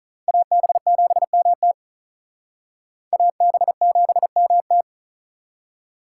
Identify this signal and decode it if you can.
Morse code, heard as a keyed tone, about 42 wpm, 700 Hz. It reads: AB7MT AB7MT
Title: Speed Racing - USA Call Signs 35wpm